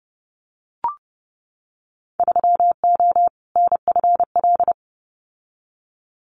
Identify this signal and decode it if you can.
Morse code, heard as a keyed tone, about 30 wpm, 700 Hz. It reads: 3O NFL
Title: Winter Field Day Exchanges 30wpm